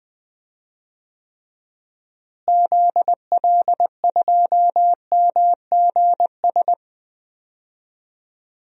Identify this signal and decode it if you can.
Morse code, heard as a keyed tone, about 20 wpm, 700 Hz. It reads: ZL2MGS
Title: DXCC-100 Call Signs - 6 Characters - Repeated 2x - 20wpm